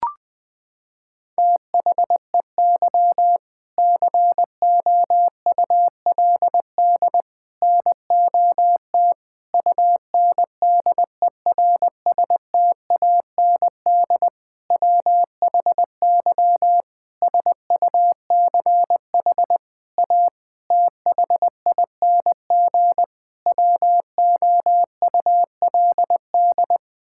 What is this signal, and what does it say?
Morse code, heard as a keyed tone, about 20 wpm, 700 Hz. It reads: THEY COULD NOT UNDERSTAND WHY SUCH A THING WOULD
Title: Sentences from Top 1000 Words - Review Part 6 - 20wpm